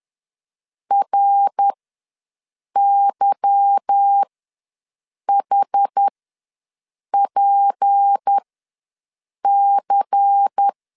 Morse code, heard as a keyed tone, about 10 wpm, spaced as for 6 wpm, 800 Hz. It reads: RYHPC